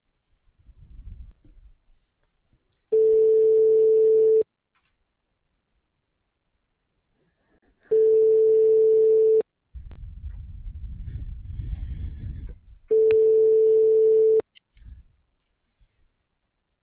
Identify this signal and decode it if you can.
Ringback tone (France), 440 Hz.